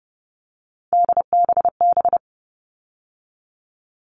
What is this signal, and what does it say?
Morse code, heard as a keyed tone, about 30 wpm, 700 Hz. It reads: DBB
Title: All Cut Numbers - 3-Digit 30wpm